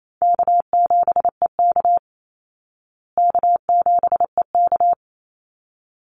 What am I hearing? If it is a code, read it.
Morse code, heard as a keyed tone, about 28 wpm, 700 Hz. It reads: K7EK K7EK
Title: Easy Call Signs Repeated 2x - 28wpm